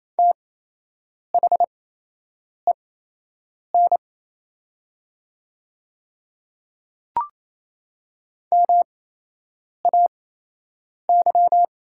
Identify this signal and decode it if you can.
Morse code, heard as a keyed tone, about 27 wpm, spaced as for 8 wpm, 700 Hz. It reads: THEN MAY
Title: 8x Character Spacing - Top 200 Words - 28wpm